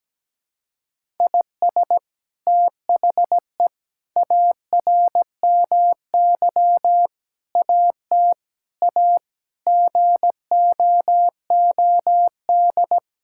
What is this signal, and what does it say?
Morse code, heard as a keyed tone, about 17 wpm, 700 Hz. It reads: IS THE ARMY AT A GOOD
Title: Sentences from Top 1000 Words - Review Part 6 - 17wpm